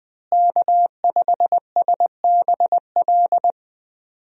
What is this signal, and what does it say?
Morse code, heard as a keyed tone, about 20 wpm, 700 Hz. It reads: K5SBL
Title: USA Call Signs - 5 Characters - Repeated 2x - 20wpm